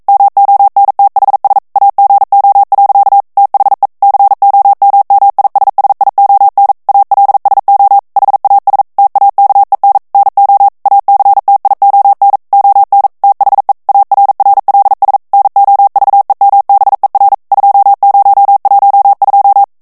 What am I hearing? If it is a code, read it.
Morse code, heard as a keyed tone, about 42 wpm, 800 Hz. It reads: MONTHS AGO. THE COMMISSION ALSO HAS TAKEN NO ACTION ON THE ARRLS NOVEMBER 2012